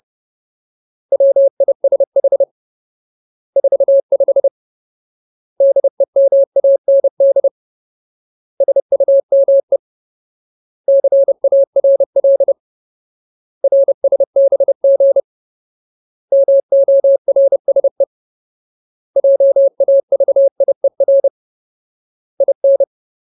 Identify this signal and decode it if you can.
Morse code, heard as a keyed tone, about 30 wpm, 550 Hz. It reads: WISH 45 DEMAND SUME CARL RSBG MORSE JAVIER IN
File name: Ultimate Word List / 30WPM Ultimate